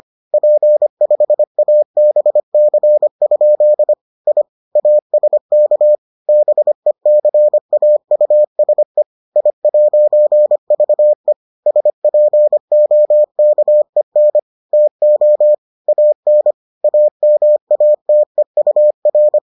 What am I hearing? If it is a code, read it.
Morse code, heard as a keyed tone, about 25 wpm, 600 Hz. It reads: P5ABC? I ASK BECAUSE I'VE SPOKEN TO AN AMATEUR